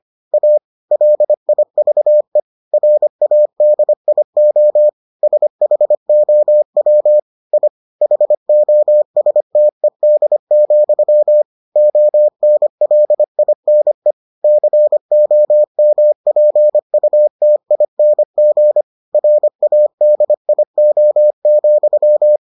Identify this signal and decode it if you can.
Morse code, heard as a keyed tone, about 25 wpm, 600 Hz. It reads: A LIVE RADIO SHOW I HOSTED, ONLINE COMPUTING RADIO,